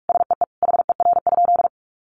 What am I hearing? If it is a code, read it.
Morse code, heard as a keyed tone, about 46 wpm, 700 Hz. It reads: SEE HER?
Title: Sentences from Top 1500 Words 45wpm